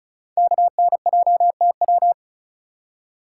Morse code, heard as a keyed tone, about 36 wpm, 700 Hz. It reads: KNJTW
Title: Code Groups - 5 Letter:Numbers - 35wpm